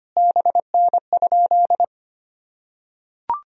Morse code, heard as a keyed tone, about 25 wpm, 700 Hz. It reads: BN?
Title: Sentences from Top 400 Words and Abbreviations 25wpm